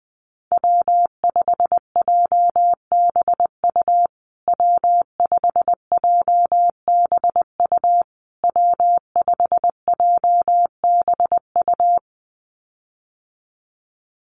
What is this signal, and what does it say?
Morse code, heard as a keyed tone, about 20 wpm, 700 Hz. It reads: W5JBU W5JBU W5JBU
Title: USA Call Signs Repeated 3x 20wpm